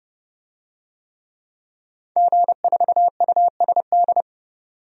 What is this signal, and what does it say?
Morse code, heard as a keyed tone, about 30 wpm, 700 Hz. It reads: G4USD
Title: DXCC-100 Call Signs - 5 Characters - 30wpm